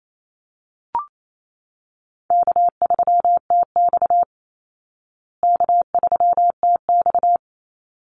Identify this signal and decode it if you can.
Morse code, heard as a keyed tone, about 28 wpm, 700 Hz. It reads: K3TX K3TX
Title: Easy Call Signs Repeated 2x - 28wpm